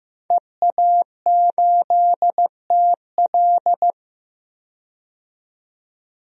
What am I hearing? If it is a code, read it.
Morse code, heard as a keyed tone, about 15 wpm, 700 Hz. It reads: EA8TL
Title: DXCC-100 Call Signs - 5 Characters - 15wpm